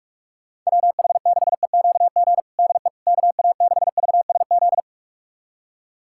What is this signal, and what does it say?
Morse code, heard as a keyed tone, about 43 wpm, 700 Hz. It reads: WH6EQG DE KA6VSZ
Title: USA Callsign DE Callsign - 6 Characters - 45wpm